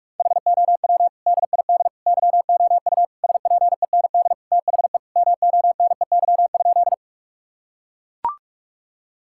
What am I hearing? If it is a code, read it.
Morse code, heard as a keyed tone, about 42 wpm, 700 Hz. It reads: HOW DID YOU SPEND THE MONEY?
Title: Sentences from Top 1000 Words - Review Part 5 - 45wpm